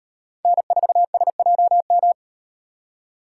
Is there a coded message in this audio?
Yes: Morse code from 0.5 to 2.1 s, heard as a keyed tone, about 38 wpm, 700 Hz. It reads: N4SJM